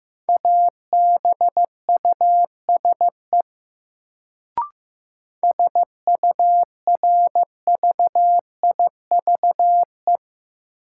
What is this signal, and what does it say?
Morse code, heard as a keyed tone, about 15 wpm, 700 Hz. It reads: ABUSE SURVIVE
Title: Sentences from Top 1300 Words - New Word Drill - 15wpm